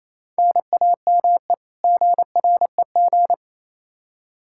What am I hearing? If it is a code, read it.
Morse code, heard as a keyed tone, about 28 wpm, 700 Hz. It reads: NAME GREG
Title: Practice QSO - Exchange 1 - Beginning - 28wpm